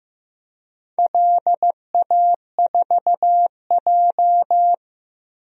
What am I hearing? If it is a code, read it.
Morse code, heard as a keyed tone, about 15 wpm, 700 Hz. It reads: LA4J
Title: DXCC-100 Call Signs - 4 Characters - 15wpm